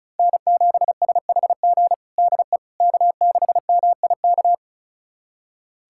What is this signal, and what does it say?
Morse code, heard as a keyed tone, about 34 wpm, 700 Hz. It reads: N7SHG DE K6MIK
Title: USA Callsign DE Callsign - 5 Characters - 35wpm